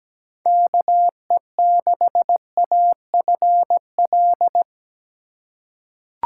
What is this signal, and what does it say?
Morse code, heard as a keyed tone, about 17 wpm, 700 Hz. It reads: KE6AFL
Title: USA Call Signs - 6 Characters - 2x Character Spacing - 17wpm